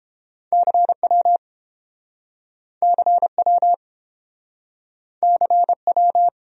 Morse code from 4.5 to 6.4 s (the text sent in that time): CW